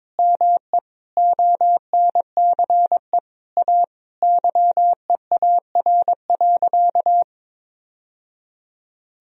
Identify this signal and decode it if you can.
Morse code, heard as a keyed tone, about 22 wpm, 700 Hz. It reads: ME ONCE A YEAR.
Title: Sentences from Top 1000 Words - Review Part 1 - 22wpm